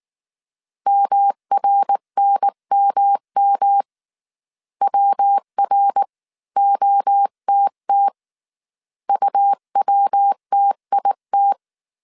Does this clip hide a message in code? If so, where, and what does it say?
Yes: Morse code from 0.9 to 11.5 s, heard as a keyed tone, about 19 wpm, 800 Hz. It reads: MRNMM WROTT UWTIT